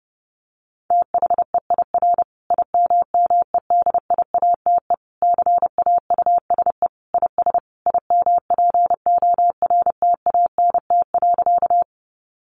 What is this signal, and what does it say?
Morse code, heard as a keyed tone, about 30 wpm, 700 Hz. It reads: THEIR IMMEDIATE CAUSE IS IMPORTANT.
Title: Sentences from Top 1200 Words 30wpm